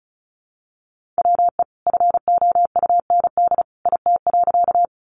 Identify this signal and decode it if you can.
Morse code, heard as a keyed tone, about 34 wpm, 700 Hz. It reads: WE FOUND IT.
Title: Sentences from Top 300 Words 35wpm